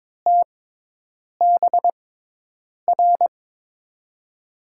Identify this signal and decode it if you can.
Morse code, heard as a keyed tone, about 22 wpm, spaced as for 8 wpm, 700 Hz. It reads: TBR
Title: USA Call Signs - 5 Characters - 6x Character Spacing - 22wpm